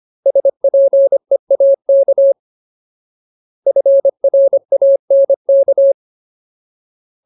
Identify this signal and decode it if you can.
Morse code, heard as a keyed tone, about 25 wpm, 550 Hz. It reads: SPEAK FRANK
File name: Ultimate Word List / 25WPM Ultimate